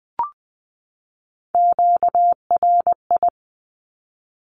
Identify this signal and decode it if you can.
Morse code, heard as a keyed tone, about 20 wpm, 700 Hz. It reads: QRI
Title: Q-Codes 20wpm